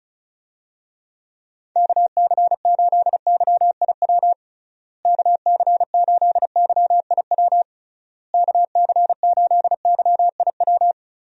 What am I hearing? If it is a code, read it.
Morse code, heard as a keyed tone, about 34 wpm, 700 Hz. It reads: KC8YIW KC8YIW KC8YIW
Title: USA Call Signs - 6 Characters - Repeated 3x - 35wpm